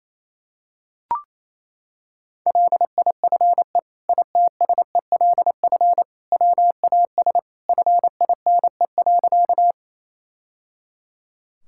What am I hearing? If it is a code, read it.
Morse code, heard as a keyed tone, about 28 wpm, 700 Hz. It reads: LIFE ITSELF WAS FINE.
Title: Sentences from Top 1000 Words - Review Part 5 - 28wpm